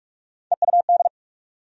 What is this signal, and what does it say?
Morse code, heard as a keyed tone, about 43 wpm, 700 Hz. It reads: EUD
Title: USA Call Sign Postfixes - 3 Letters 45wpm